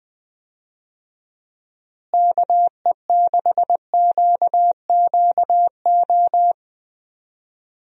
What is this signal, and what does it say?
Morse code, heard as a keyed tone, about 20 wpm, 700 Hz. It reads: KE6QQO
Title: USA Call Signs - 6 Characters - Repeated 3x - 20wpm